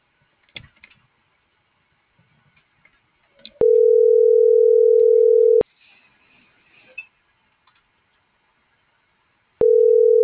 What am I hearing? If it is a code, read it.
Ringback tone (North America), 440 + 480 Hz.